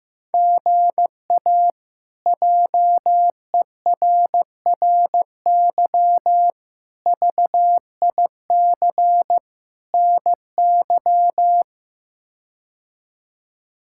Morse code, heard as a keyed tone, about 15 wpm, 700 Hz. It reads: GA JERRY VIC NY
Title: K1USN SST QSO Practice - 15wpm